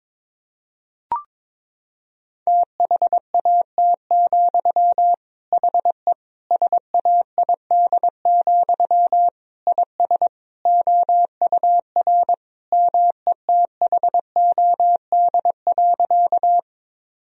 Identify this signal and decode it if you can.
Morse code, heard as a keyed tone, about 22 wpm, 700 Hz. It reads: THAT, HE SAID, IS OUR METHOD.
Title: Sentences from Top 1000 Words - Review Part 3 - 22wpm